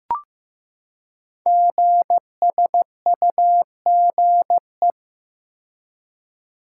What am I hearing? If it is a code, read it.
Morse code, heard as a keyed tone, about 15 wpm, 700 Hz. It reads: GSUGE
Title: Code Groups - 5 Letters - 15wpm